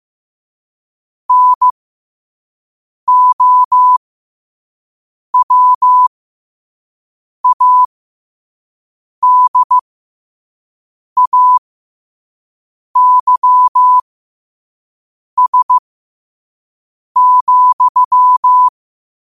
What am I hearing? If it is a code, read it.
Morse code, heard as a keyed tone, about 15 wpm, spaced as for 5 wpm, 1000 Hz. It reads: NOWADAYS,
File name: qotdfiles / 5wpm